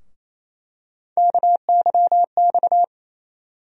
Morse code, heard as a keyed tone, about 28 wpm, 700 Hz. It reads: KYX
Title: USA Call Sign Postfixes - 3 Letters 28wpm